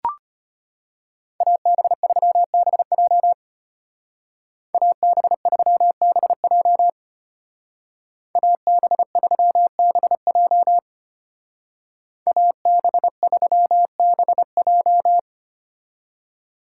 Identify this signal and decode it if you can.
Morse code, heard as a keyed tone, about 30 wpm, 700 Hz. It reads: AB3BJ AB3BJ AB3BJ AB3BJ
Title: Speed Racing - USA 2x2 Call Signs 25wpm